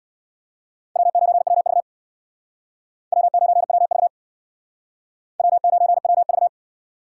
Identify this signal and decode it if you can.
Morse code, heard as a keyed tone, about 65 wpm, 690 Hz. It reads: W9WV W9WV W9WV